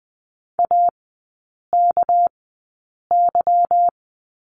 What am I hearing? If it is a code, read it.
Morse code, heard as a keyed tone, about 20 wpm, spaced as for 8 wpm, 700 Hz. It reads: AKY
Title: 198 Lesson 32e - Letter k - Three Characters 20wpm